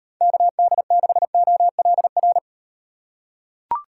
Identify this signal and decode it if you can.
Morse code, heard as a keyed tone, about 37 wpm, 700 Hz. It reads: KD6OLR